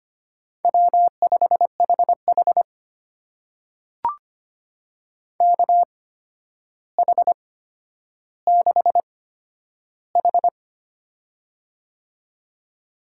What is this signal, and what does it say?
Morse code, heard as a keyed tone, about 25 wpm, spaced as for 11 wpm, 700 Hz. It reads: W5HH KH6H
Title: USA Call Signs - 4 Characters - 8x Character Spacing - 25wpm